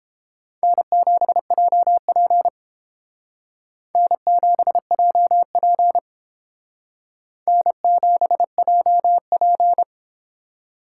Morse code, heard as a keyed tone, about 28 wpm, 700 Hz. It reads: N7JP N7JP N7JP